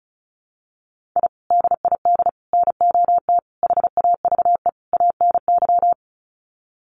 Morse code, heard as a keyed tone, about 34 wpm, 700 Hz. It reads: I DID NOT HAVE ANY
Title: Sets of 5 Words Encore 35wpm